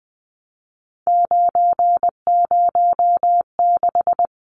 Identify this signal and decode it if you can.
Morse code, heard as a keyed tone, about 20 wpm, 700 Hz. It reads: 906
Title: Numbers - 3-Digit 20wpm